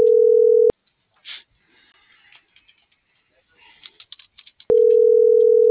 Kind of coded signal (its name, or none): telephone call progress tone